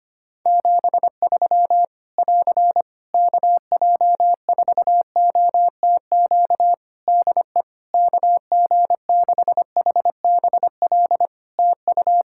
Morse code, heard as a keyed tone, about 25 wpm, 700 Hz. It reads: 73 <AR> KJ4OTQ DE KG6HBL TU
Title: Practice QSO - Exchange 4 - Ending - 25wpm